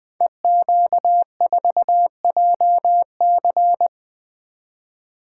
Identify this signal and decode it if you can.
Morse code, heard as a keyed tone, about 20 wpm, 700 Hz. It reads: EQ4JC
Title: Code Groups - 5 Letter:Numbers - 20wpm